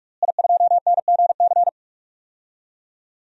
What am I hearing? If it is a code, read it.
Morse code, heard as a keyed tone, about 42 wpm, 700 Hz. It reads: I2NGC
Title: DXCC-100 Call Signs - 5 Characters - Repeated 3x - 45wpm